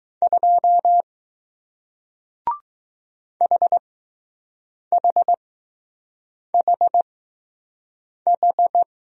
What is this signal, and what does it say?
Morse code, heard as a keyed tone, about 20 wpm, spaced as for 6 wpm, 700 Hz. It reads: 2 HHHH